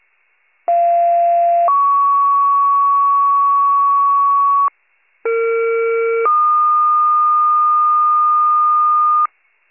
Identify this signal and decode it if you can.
Two-tone page, heard as a one-second tone, then a three-second tone, about 690 then 1120 Hz, about 460 then 1220 Hz. It is heard from 0.7 to 9.3 s.